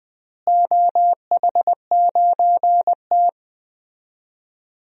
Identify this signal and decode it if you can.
Morse code, heard as a keyed tone, about 20 wpm, 700 Hz. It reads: OH9T
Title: DXCC-100 Call Signs - 4 Characters - Repeated 2x - 20wpm